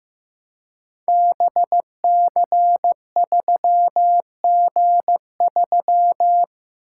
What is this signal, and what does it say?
Morse code, heard as a keyed tone, about 15 wpm, 700 Hz. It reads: BC3G3
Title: Code Groups - 5 Letter:Numbers - 15wpm